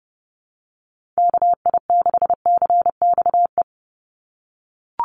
Morse code, heard as a keyed tone, about 30 wpm, 700 Hz. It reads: KI6CXE
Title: USA Call Signs 30wpm